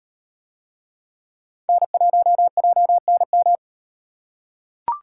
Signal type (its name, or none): Morse code